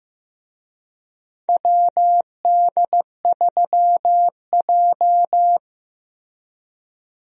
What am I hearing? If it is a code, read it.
Morse code, heard as a keyed tone, about 15 wpm, 700 Hz. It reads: WD3J
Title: USA Call Signs - 4 Characters - Repeated 2x - 15wpm